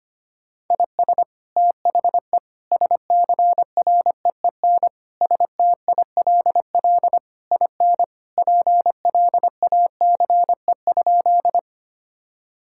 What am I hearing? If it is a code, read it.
Morse code, heard as a keyed tone, about 25 wpm, 700 Hz. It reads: IS THE SCREEN STILL IN PLACE?
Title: Sentences from Top 1000 Words - Review Part 6 - 25wpm